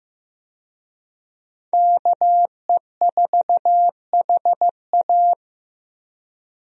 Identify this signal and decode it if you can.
Morse code, heard as a keyed tone, about 15 wpm, 700 Hz. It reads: KE4HA
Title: USA Call Signs - 5 Characters - Repeated 3x - 15wpm